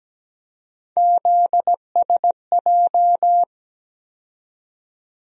Morse code, heard as a keyed tone, about 17 wpm, 700 Hz. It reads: ZSJ
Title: USA Call Sign Postfixes - 3 Letters 17wpm